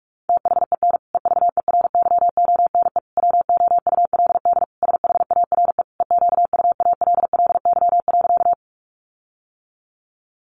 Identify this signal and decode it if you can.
Morse code, heard as a keyed tone, about 44 wpm, 700 Hz. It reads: THEN EVERYONE WOULD SHARE EQUALLY.
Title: Sentences from Top 1400 Words 45wpm